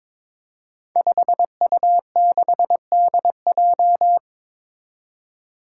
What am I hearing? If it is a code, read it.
Morse code, heard as a keyed tone, about 22 wpm, 700 Hz. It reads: 5U6DJ
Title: Five Challenging Letter-Number Combinations 22wpm